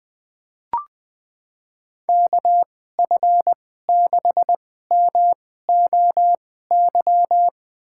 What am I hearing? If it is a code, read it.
Morse code, heard as a keyed tone, about 21 wpm, spaced as for 15 wpm, 700 Hz. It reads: KF6MOY
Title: USA Call Signs - 6 Characters - 2x Character Spacing - 20wpm